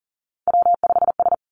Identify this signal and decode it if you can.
Morse code, heard as a keyed tone, about 39 wpm, 700 Hz. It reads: W5S
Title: Three Challenging Letter-Number Combinations 40wpm